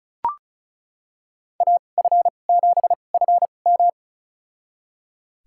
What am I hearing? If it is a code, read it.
Morse code, heard as a keyed tone, about 36 wpm, spaced as for 25 wpm, 700 Hz. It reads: AF7FM
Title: USA Call Signs - 5 Characters - 2x Character Spacing - 35wpm